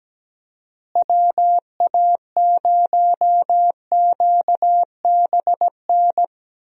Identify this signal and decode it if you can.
Morse code, heard as a keyed tone, about 17 wpm, 700 Hz. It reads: WA0QBN
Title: USA Call Signs - 6 Characters - Repeated 3x - 17wpm